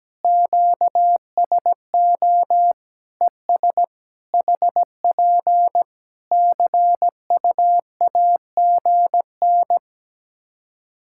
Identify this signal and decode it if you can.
Morse code, heard as a keyed tone, about 17 wpm, 700 Hz. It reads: QSO ES HP CUAGN
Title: Practice QSO - Exchange 4 - Ending - 17wpm